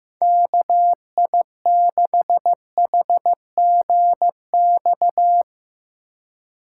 Morse code, heard as a keyed tone, about 16 wpm, 700 Hz. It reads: KI6HGX
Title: Speed Racing - USA Call Signs 15wpm